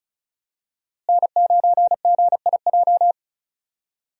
Morse code, heard as a keyed tone, about 34 wpm, 700 Hz. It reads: N9GIJ
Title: USA Call Signs - 5 Characters - Repeated 2x - 35wpm